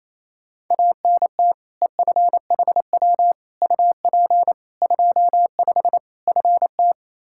Morse code, heard as a keyed tone, about 28 wpm, 700 Hz. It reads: ANT EFHW UP 25 FT
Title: K1USN SST QSO Practice 28wpm